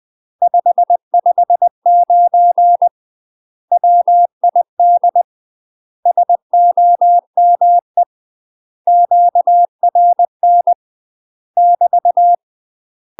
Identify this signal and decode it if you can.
Morse code, heard as a keyed tone, about 20 wpm, 700 Hz. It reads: 559 WID SOME QRN <BT>
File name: UK MASTER QSOs TRAINING / 20WPM / UK MASTER QSOs 0060